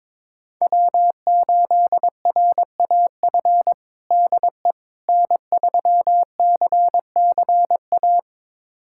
Morse code, heard as a keyed tone, about 22 wpm, 700 Hz. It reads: W8RAF DE N3CCA
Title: USA Callsign DE Callsign - 5 Characters - 22wpm